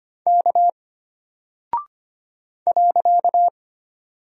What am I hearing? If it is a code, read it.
Morse code, heard as a keyed tone, about 25 wpm, 700 Hz. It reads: K .
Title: Single Character (Letter:Number:Punctuation:Prosign) Alt 25wpm